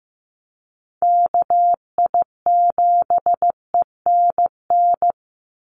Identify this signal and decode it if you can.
Morse code, heard as a keyed tone, about 15 wpm, 700 Hz. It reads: KI7ENN
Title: USA Call Signs 15wpm